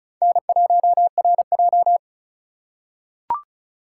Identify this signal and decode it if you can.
Morse code, heard as a keyed tone, about 34 wpm, 700 Hz. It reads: N1RJ
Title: USA Call Signs - 4 Characters - 3x Character Spacing - 35wpm